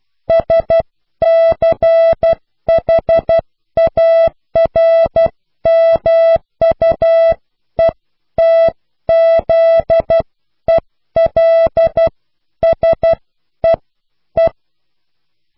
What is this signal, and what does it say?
Morse code, heard as a keyed tone, about 12 wpm, spaced as for 10 wpm, 660 Hz. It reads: SCHARMUETZELSEE
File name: cw / Pseudo woorden (15)